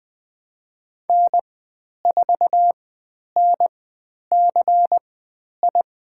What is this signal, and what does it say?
Morse code, heard as a keyed tone, about 20 wpm, spaced as for 10 wpm, 700 Hz. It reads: N4NCI
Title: 109 Lesson 18g - Letter m - Call Signs 20wpm